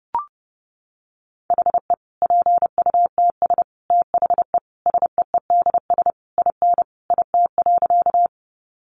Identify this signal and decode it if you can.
Morse code, heard as a keyed tone, about 30 wpm, 700 Hz. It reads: HE PUTS THE SEEDS IN IT.
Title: Easy Phrases 30wpm